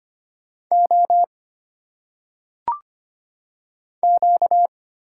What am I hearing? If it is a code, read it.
Morse code, heard as a keyed tone, about 25 wpm, 700 Hz. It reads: O Q
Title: Single Letters - Phonetic Pronunciation 25wpm